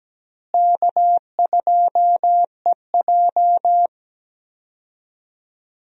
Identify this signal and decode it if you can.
Morse code, heard as a keyed tone, about 17 wpm, 700 Hz. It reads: K2EJ
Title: USA Call Signs - 4 Characters - Repeated 3x - 17wpm